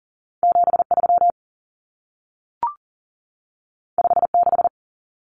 Morse code, heard as a keyed tone, about 40 wpm, 700 Hz. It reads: 73 56
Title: Numbers - 2-Digit 40wpm